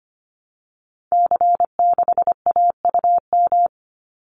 Morse code, heard as a keyed tone, about 25 wpm, 700 Hz. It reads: C6AUM
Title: Call Signs Repeated 3x - 25wpm